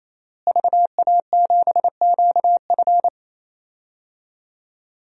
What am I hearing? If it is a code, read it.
Morse code, heard as a keyed tone, about 28 wpm, 700 Hz. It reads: VA7QF
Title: DXCC-100 Call Signs - 5 Characters - Repeated 3x - 28wpm